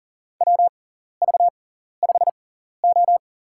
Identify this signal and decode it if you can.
Morse code, heard as a keyed tone, about 40 wpm, spaced as for 14 wpm, 700 Hz. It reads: WV5O
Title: USA Call Signs - 6x Character Spacing - 40wpm